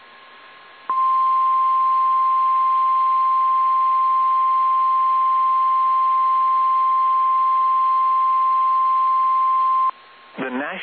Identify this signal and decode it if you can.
Emergency Alert System (SAME) tones (the 1050 Hz attention tone), with speech. It is heard from 0.8 to 9.9 s.
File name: eas / KSDZKDJL / KSDZKDJL